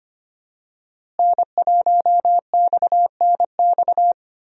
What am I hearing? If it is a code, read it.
Morse code, heard as a keyed tone, about 25 wpm, 700 Hz. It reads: N1XNX
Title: USA Call Signs - 5 Characters - Repeated 3x - 25wpm